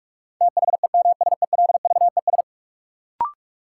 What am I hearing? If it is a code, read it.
Morse code, heard as a keyed tone, about 42 wpm, 700 Hz. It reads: THEMSELVES
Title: Sentences from Top 800 Words - New Word Drill - 45wpm